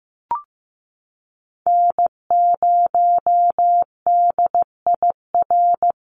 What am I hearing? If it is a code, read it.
Morse code, heard as a keyed tone, about 15 wpm, 700 Hz. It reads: N0DIR
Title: Call Signs Repeated 2x - 15wpm